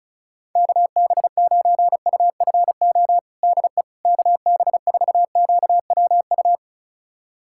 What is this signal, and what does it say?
Morse code, heard as a keyed tone, about 36 wpm, 700 Hz. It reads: KB9UFO DE KB4QWU
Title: USA Callsign DE Callsign - 6 Characters - 35wpm